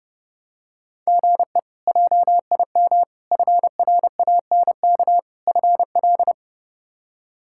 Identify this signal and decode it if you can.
Morse code, heard as a keyed tone, about 32 wpm, 700 Hz. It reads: GE JIM FRANK FL
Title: K1USN SST QSO Practice - 30wpm